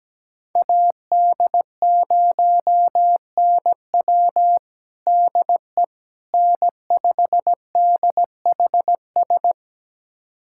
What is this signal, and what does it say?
Morse code, heard as a keyed tone, about 17 wpm, 700 Hz. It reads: AD0NW DE N5DHS
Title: USA Callsign DE Callsign - 5 Characters - 17wpm